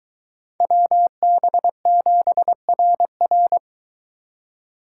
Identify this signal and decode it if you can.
Morse code, heard as a keyed tone, about 23 wpm, 700 Hz. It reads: WB7RR